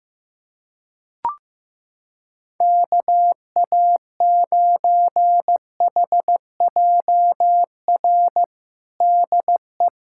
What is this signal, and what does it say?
Morse code, heard as a keyed tone, about 16 wpm, 700 Hz. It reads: KA9HJR DE
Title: Practice QSO - Exchange 4 - Ending - 15wpm